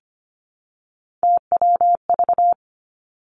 Morse code, heard as a keyed tone, about 25 wpm, 700 Hz. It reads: TWV
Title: Three Challenging Letter-Number Combinations 25wpm